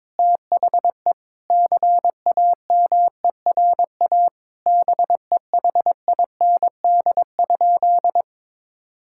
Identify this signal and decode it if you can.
Morse code, heard as a keyed tone, about 21 wpm, 700 Hz. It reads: THE CAMERA BEHIND?
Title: Sentences from Top 1000 Words - Review Part 5 - 22wpm